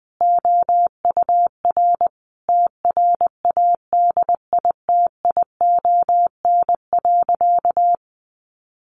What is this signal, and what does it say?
Morse code, heard as a keyed tone, about 20 wpm, 700 Hz. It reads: OUR TRADITION.
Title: Sentences from Top 1500 Words 20wpm